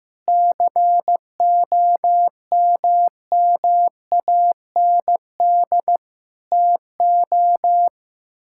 Morse code, heard as a keyed tone, about 15 wpm, 700 Hz. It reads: COMMAND TO
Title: Sentences from Top 1600 Words 15wpm